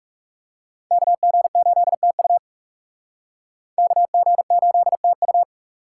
Morse code, heard as a keyed tone, about 37 wpm, 700 Hz. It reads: KG8TU KG8TU
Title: Speed Racing - USA 2x2 Call Signs 30wpm